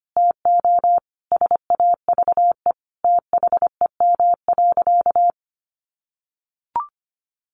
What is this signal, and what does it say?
Morse code, heard as a keyed tone, about 25 wpm, 700 Hz. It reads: TO SAVE THEM.
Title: Sentences from Top 1100 Words 25wpm